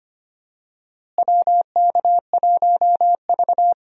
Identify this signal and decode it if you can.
Morse code, heard as a keyed tone, about 25 wpm, 700 Hz. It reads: WK1V
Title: USA Call Signs - 4 Characters - Repeated 2x - 25wpm